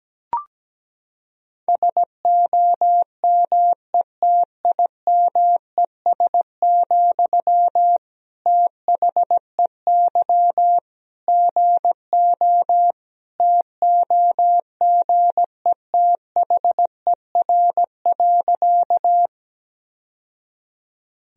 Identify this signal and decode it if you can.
Morse code, heard as a keyed tone, about 17 wpm, 700 Hz. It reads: SOMETIMES, THEY GO TOGETHER.
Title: Sentences from Top 1000 Words - Review Part 3 - 17wpm